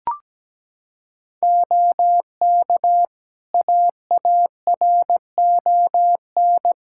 Morse code, heard as a keyed tone, about 17 wpm, 700 Hz. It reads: OK AARON
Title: K1USN SST QSO Practice 17wpm